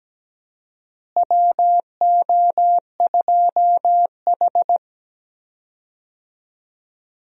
Morse code, heard as a keyed tone, about 17 wpm, 700 Hz. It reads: WO2H
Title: USA Call Signs - 4 Characters - Repeated 3x - 17wpm